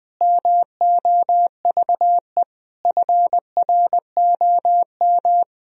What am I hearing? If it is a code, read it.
Morse code, heard as a keyed tone, about 20 wpm, 700 Hz. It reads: MOVE FROM
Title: Sentences from Top 1000 Words - Review Part 1 - 20wpm